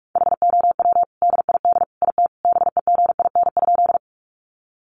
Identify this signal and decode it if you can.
Morse code, heard as a keyed tone, about 44 wpm, 700 Hz. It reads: HOW DID IT BEGIN?
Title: Sentences from Top 300 Words 45wpm